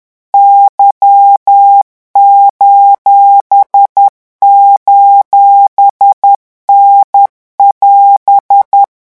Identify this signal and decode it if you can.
Morse code, heard as a keyed tone, about 10 wpm, 800 Hz. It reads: Y::N<AS>
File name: morse mp3 / revision19-vitesse 10 mots